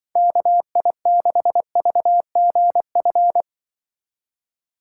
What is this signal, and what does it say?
Morse code, heard as a keyed tone, about 24 wpm, 700 Hz. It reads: KI6VGF
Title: Speed Racing - USA 2x3 Call Signs 20wpm